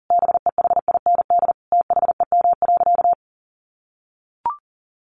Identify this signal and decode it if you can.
Morse code, heard as a keyed tone, about 40 wpm, 700 Hz. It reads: BEHIND THEM.
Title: Sentences from Top 1300 Words 40wpm